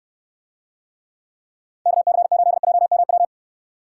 Morse code, heard as a keyed tone, about 64 wpm, 700 Hz. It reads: KC8JMW